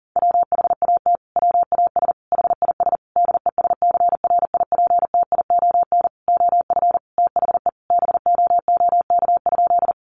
Morse code, heard as a keyed tone, about 40 wpm, 700 Hz. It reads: WHAT WAS HIS DESCRIPTION OF THE BOOK?
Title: Sentences from Top 1300 Words 40wpm